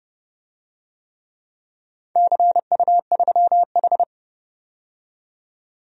Morse code, heard as a keyed tone, about 28 wpm, 700 Hz. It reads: CU3H
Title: DXCC-100 Call Signs - 4 Characters - 30wpm